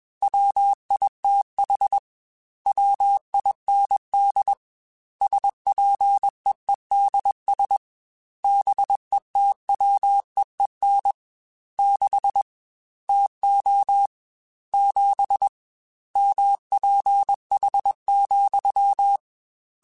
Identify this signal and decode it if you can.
Morse code, heard as a keyed tone, about 21 wpm, 800 Hz. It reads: WITH WIND SPEEDS BETWEEN 6 TO 7 MPH,
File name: Weather Sets / Weather Sets-mp3s / Weather 4 24 @20